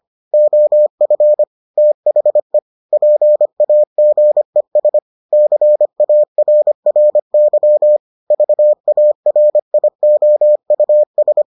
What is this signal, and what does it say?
Morse code, heard as a keyed tone, about 25 wpm, 600 Hz. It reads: OF THE PAGES CARRY VARIOUS